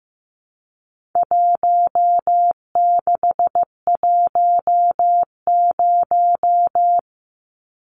Morse code, heard as a keyed tone, about 15 wpm, 700 Hz. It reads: 1610
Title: Numbers - 4-Digit 15wpm